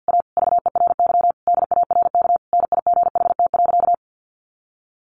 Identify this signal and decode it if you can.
Morse code, heard as a keyed tone, about 50 wpm, 700 Hz. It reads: A VERY DARK NIGHT.
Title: Sentences from Top 500 Words 50wpm